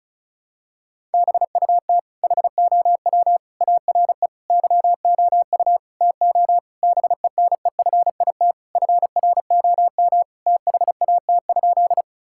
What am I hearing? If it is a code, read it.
Morse code, heard as a keyed tone, about 36 wpm, 700 Hz. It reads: BUT HOW ARE YOU TO BENEFIT FROM THAT?
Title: Sentences from Top 1000 Words - Review Part 6 - 35wpm